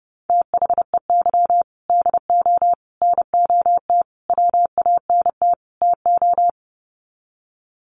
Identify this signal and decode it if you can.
Morse code, heard as a keyed tone, about 30 wpm, 700 Hz. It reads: THEY DO NOT WANT TO
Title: Sets of 5 Words 30wpm